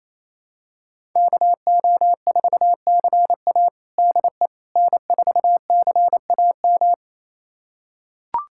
Morse code, heard as a keyed tone, about 28 wpm, 700 Hz. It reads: KO4CA DE N4CAM
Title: USA Callsign DE Callsign - 5 Characters - 28wpm